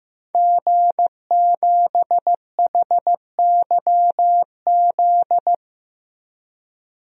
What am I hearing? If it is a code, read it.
Morse code, heard as a keyed tone, about 15 wpm, 700 Hz. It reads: G7HYZ
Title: DXCC-100 Call Signs - 5 Characters - 15wpm